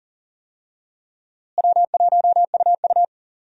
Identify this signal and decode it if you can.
Morse code, heard as a keyed tone, about 40 wpm, 700 Hz. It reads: W1UU
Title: USA Call Signs - 4 Characters - Repeated 3x - 40wpm